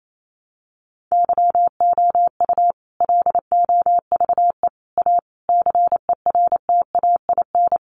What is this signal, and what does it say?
Morse code, heard as a keyed tone, about 28 wpm, 700 Hz. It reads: YOU LOVE A CERTAIN
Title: Sentences from Top 500 Words 28wpm